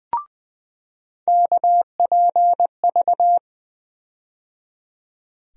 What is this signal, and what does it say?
Morse code, heard as a keyed tone, about 20 wpm, 700 Hz. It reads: KPV
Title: USA Call Sign Postfixes - 3 Letters 20wpm